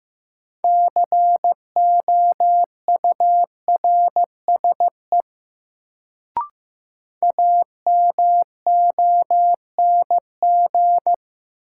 Morse code, heard as a keyed tone, about 16 wpm, 700 Hz. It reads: COURSE AMONG
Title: Sentences from Top 500 Words - New Word Drill - 15wpm